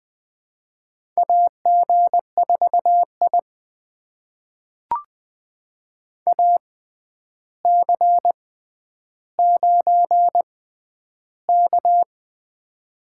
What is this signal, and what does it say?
Morse code, heard as a keyed tone, about 21 wpm, spaced as for 10 wpm, 700 Hz. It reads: AG4I AC9K
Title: USA Call Signs - 4 Characters - 6x Character Spacing - 20wpm